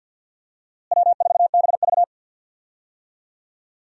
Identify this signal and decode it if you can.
Morse code, heard as a keyed tone, about 53 wpm, 700 Hz. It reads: W4BV
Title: USA Call Signs - 4 Characters - Repeated 2x - 50wpm